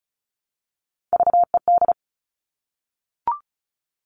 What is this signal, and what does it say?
Morse code, heard as a keyed tone, about 34 wpm, 700 Hz. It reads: VED
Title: All Cut Numbers - 3-Digit 35wpm